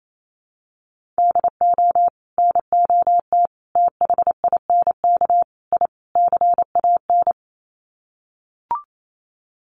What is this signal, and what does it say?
Morse code, heard as a keyed tone, about 28 wpm, 700 Hz. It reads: DO NOT THINK I CAN
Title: Sets of 5 Words Encore2 28wpm